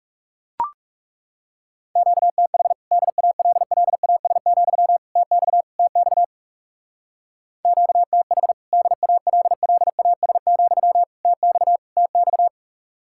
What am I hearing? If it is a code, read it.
Morse code, heard as a keyed tone, about 37 wpm, 700 Hz. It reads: QTH DALLAS, TX TX QTH DALLAS, TX TX
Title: Speed-Racing Common QSO Phrases - 30wpm